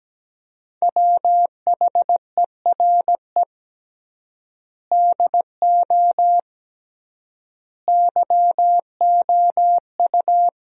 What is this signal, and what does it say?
Morse code, heard as a keyed tone, about 17 wpm, 700 Hz. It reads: WHERE DO YOU
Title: 3x Word-Spacing - Sentences from Top 200 Words - 17wpm